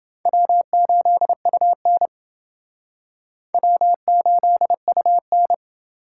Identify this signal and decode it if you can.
Morse code, heard as a keyed tone, about 28 wpm, 700 Hz. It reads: W8UN W8UN